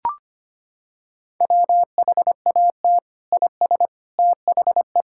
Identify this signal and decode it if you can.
Morse code, heard as a keyed tone, about 25 wpm, 700 Hz. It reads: WHAT IS THE
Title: Sentences from Top 1000 Words - Review Part 5 - 25wpm